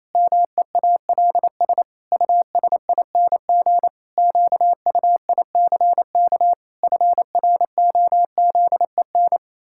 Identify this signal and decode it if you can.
Morse code, heard as a keyed tone, about 27 wpm, 700 Hz. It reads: MEALS USING QUICK FROZEN
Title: Pangram Challenge v2 - 28wpm